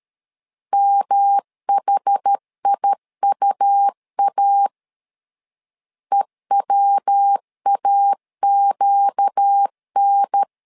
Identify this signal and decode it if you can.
Morse code, heard as a keyed tone, about 13 wpm, 800 Hz. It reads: MHIUA EWAQN
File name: LEZIONI / 4 Q 12wpm 1833